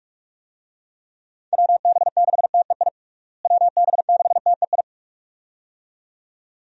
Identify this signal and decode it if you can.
Morse code, heard as a keyed tone, about 46 wpm, 700 Hz. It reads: WB6TEI WB6TEI
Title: USA Call Signs - 6 Characters - Repeated 2x - 45wpm